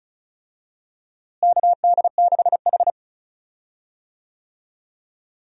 Morse code, heard as a keyed tone, about 34 wpm, 700 Hz. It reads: KD6H
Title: USA Call Signs - 4 Characters - Repeated 3x - 35wpm